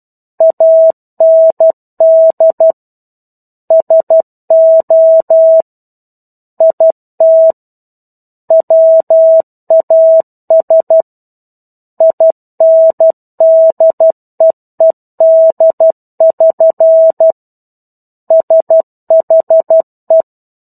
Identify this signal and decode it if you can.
Morse code, heard as a keyed tone, about 12 wpm, 650 Hz. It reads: AND SO IT WAS INDEED<SN> SHE
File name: 01-Chapter-01 12 WPM